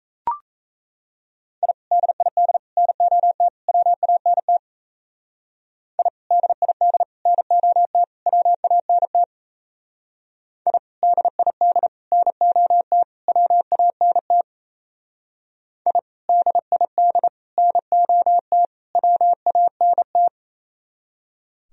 Morse code, heard as a keyed tone, about 36 wpm, 700 Hz. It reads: I DID NOT WANT I DID NOT WANT I DID NOT WANT I DID NOT WANT